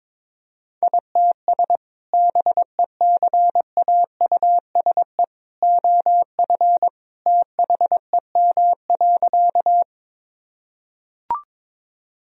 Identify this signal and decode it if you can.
Morse code, heard as a keyed tone, about 22 wpm, 700 Hz. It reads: ITS BECAUSE OF THEM.
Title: Sentences from Top 1000 Words - Review Part 3 - 22wpm